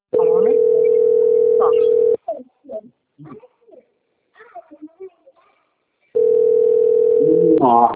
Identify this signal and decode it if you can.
Ringback tone (North America), 440 + 480 Hz, with speech.
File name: out- - - - - .36902